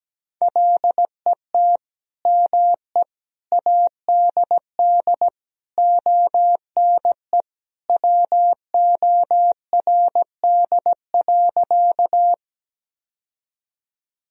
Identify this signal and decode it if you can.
Morse code, heard as a keyed tone, about 17 wpm, 700 Hz. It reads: LET ME ADD ONE WORD.
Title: Sentences from Top 1000 Words - Review Part 1 - 17wpm